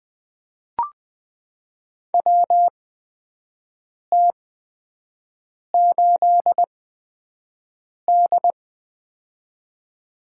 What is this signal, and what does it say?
Morse code, heard as a keyed tone, about 20 wpm, spaced as for 5 wpm, 700 Hz. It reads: WT8D
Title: USA Call Signs - 4 Characters - 8x Character Spacing - 20wpm